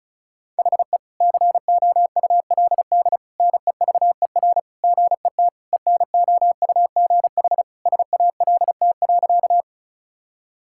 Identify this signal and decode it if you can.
Morse code, heard as a keyed tone, about 36 wpm, 700 Hz. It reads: HE COULD NEVER GET ENOUGH SALT.
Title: Sentences from Top 1000 Words - Review Part 5 - 35wpm